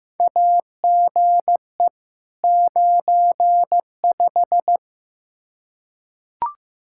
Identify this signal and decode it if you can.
Morse code, heard as a keyed tone, about 15 wpm, 700 Hz. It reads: AGE 95
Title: K1USN SST QSO Practice 15wpm